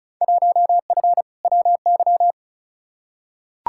Morse code, heard as a keyed tone, about 36 wpm, 700 Hz. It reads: 1F WY
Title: Field Day Exchanges Encore - 35wpm